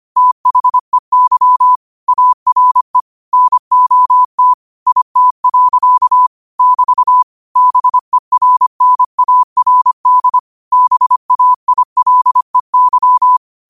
Morse code, heard as a keyed tone, about 25 wpm, 1000 Hz. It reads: THEY ARE NOT IT. <BT> BERNARD BAILEY
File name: qotdfiles / 25wpm